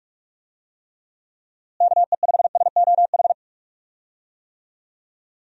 Morse code, heard as a keyed tone, about 46 wpm, 700 Hz. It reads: KE5SOH
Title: USA Call Signs - 6 Characters - Repeated 3x - 45wpm